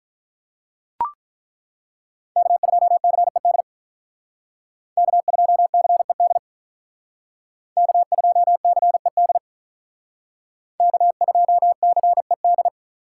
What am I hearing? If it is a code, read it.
Morse code, heard as a keyed tone, about 43 wpm, 700 Hz. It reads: K2CED K2CED K2CED K2CED
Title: Speed Racing - USA Call Signs 35wpm